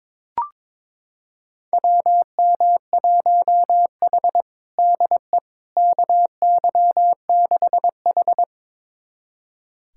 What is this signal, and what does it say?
Morse code, heard as a keyed tone, about 22 wpm, 700 Hz. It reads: WM1H DE KY6H
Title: USA Callsign DE Callsign - 4 Characters - 22wpm